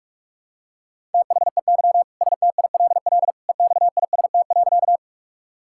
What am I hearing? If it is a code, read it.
Morse code, heard as a keyed tone, about 42 wpm, 700 Hz. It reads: THEY STILL EXIST.
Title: Sentences from Top 1000 Words - Review Part 5 - 45wpm